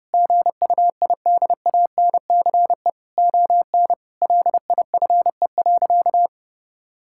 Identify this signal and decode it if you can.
Morse code, heard as a keyed tone, about 30 wpm, 700 Hz. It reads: GUIDANCE ON LIFE.
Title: Sentences from Top 1600 Words 30wpm